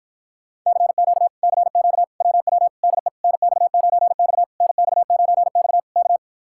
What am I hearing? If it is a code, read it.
Morse code, heard as a keyed tone, about 56 wpm, 700 Hz. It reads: CQ CQ WW DE NX9X NX9X K